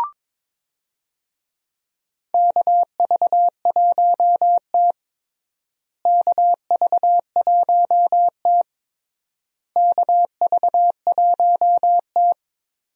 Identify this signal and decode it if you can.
Morse code, heard as a keyed tone, about 22 wpm, 700 Hz. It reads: KV1T KV1T KV1T
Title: USA Call Signs - 4 Characters - Repeated 3x - 22wpm